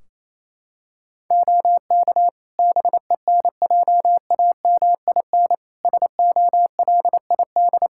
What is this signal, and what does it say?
Morse code, heard as a keyed tone, about 28 wpm, 700 Hz. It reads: OK BENJAMIN SOLID
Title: Practice QSO - Exchange 3 - More Info - 28wpm